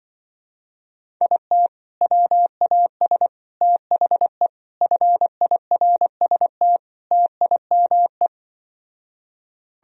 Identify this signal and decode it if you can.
Morse code, heard as a keyed tone, about 24 wpm, 700 Hz. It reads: IT WAS THE FIRST TIME
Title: Speed-Racing - Sets of 5 Words 20wpm